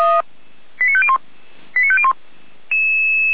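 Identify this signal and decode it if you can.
Five-tone selective call (ZVEI), heard as quick runs of five tones, tones of about 70 ms. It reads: 89501 89501
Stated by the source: ZVEI five-tone alert received by SDR.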